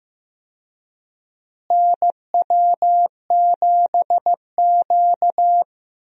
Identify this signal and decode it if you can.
Morse code, heard as a keyed tone, about 15 wpm, 700 Hz. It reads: NW7Q
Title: USA Call Signs - 4 Characters - Repeated 3x - 15wpm